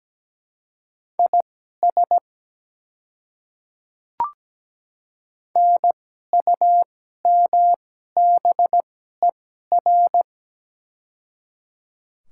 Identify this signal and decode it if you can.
Morse code, heard as a keyed tone, about 17 wpm, spaced as for 12 wpm, 690 Hz. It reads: IS NUMBER
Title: 2x Character Spacing - Top 100 Words - 17wpm